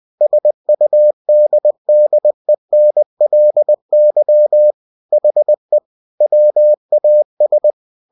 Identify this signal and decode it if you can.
Morse code, heard as a keyed tone, about 20 wpm, 600 Hz. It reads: SUDDENLY HE WAS
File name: cw-books / winnie-the-pooh / winnie-20wpm-page-0093